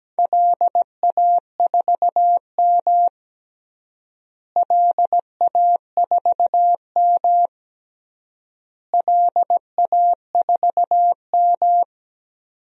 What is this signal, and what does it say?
Morse code, heard as a keyed tone, about 17 wpm, 700 Hz. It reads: LA4M LA4M LA4M
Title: DXCC-100 Call Signs - 4 Characters - Repeated 3x - 17wpm